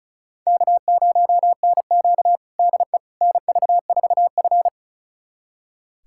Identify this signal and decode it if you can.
Morse code, heard as a keyed tone, about 36 wpm, 700 Hz. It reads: K0NQ DE NV4F
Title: USA Callsign DE Callsign - 4 Characters - 35wpm